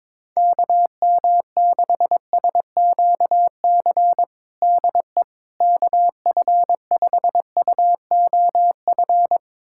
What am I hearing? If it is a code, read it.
Morse code, heard as a keyed tone, about 22 wpm, 700 Hz. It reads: KM6SQC DE KF5UOF
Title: USA Callsign DE Callsign - 6 Characters - 22wpm